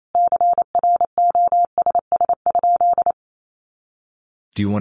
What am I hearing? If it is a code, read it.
Morse code, heard as a keyed tone, about 28 wpm, 700 Hz. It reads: CROSS?
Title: Sentences from Top 300 Words 28wpm